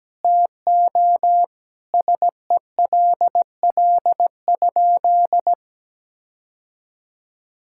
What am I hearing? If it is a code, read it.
Morse code, heard as a keyed tone, about 17 wpm, 700 Hz. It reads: TO SELL?
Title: Sentences from Top 1600 Words 17wpm